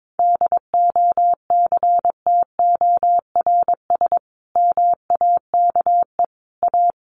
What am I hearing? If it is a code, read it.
Morse code, heard as a keyed tone, about 22 wpm, 700 Hz. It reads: DOCTORS MAKE A
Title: Easy Phrases 22wpm